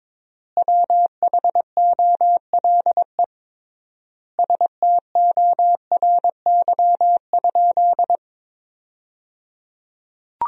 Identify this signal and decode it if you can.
Morse code, heard as a keyed tone, about 22 wpm, 700 Hz. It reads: WHOLE STORY?
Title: 3x Word-Spacing - Sentences from Top 400 Words - 22wpm